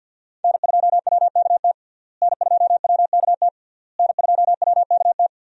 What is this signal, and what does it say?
Morse code, heard as a keyed tone, about 52 wpm, 700 Hz. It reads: N2WKT N2WKT N2WKT
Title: USA Call Signs - 5 Characters - Repeated 3x - 50wpm